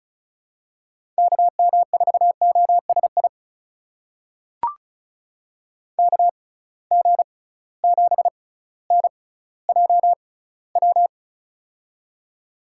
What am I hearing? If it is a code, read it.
Morse code, heard as a keyed tone, about 36 wpm, spaced as for 18 wpm, 700 Hz. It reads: KM4OSI KG7NJW
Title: USA Call Signs - 6 Characters - 6x Character Spacing - 35wpm